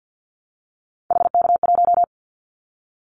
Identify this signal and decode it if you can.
Morse code, heard as a keyed tone, about 50 wpm, 700 Hz. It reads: HK1
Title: Callsign Prefixes for Top 100 DXCC - 50wpm